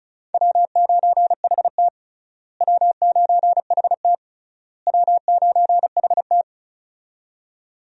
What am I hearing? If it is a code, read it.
Morse code, heard as a keyed tone, about 36 wpm, 700 Hz. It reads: W9HT W9HT W9HT
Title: USA Call Signs - 4 Characters - Repeated 3x - 35wpm